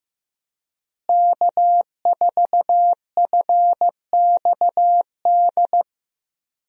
Morse code, heard as a keyed tone, about 15 wpm, 700 Hz. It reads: K4FXD
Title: USA Call Signs - 5 Characters - Repeated 3x - 15wpm